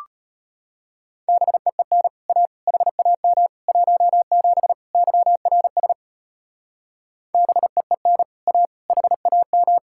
Morse code, heard as a keyed tone, about 30 wpm, 700 Hz. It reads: BEEN A HAM 17 YRS BEEN A HAM
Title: Speed-Racing Common QSO Phrases Encore - 25wpm